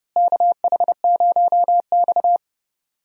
Morse code, heard as a keyed tone, about 32 wpm, 700 Hz. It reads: KH0X
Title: USA Call Signs - 4 Characters - Repeated 2x - 30wpm